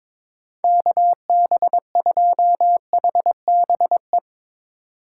Morse code, heard as a keyed tone, about 22 wpm, 700 Hz. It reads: KB2HBE
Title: USA Call Signs - 4x Character Spacing - 22wpm